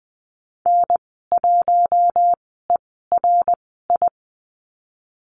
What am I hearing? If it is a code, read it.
Morse code, heard as a keyed tone, about 20 wpm, spaced as for 15 wpm, 700 Hz. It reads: N1ERI
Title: 069 Lesson 12g - Letter d - Call Signs 20wpm